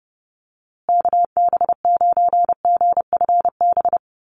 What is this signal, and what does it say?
Morse code, heard as a keyed tone, about 30 wpm, 700 Hz. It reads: KB9GFB
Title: USA Call Signs Repeated 2x 30wpm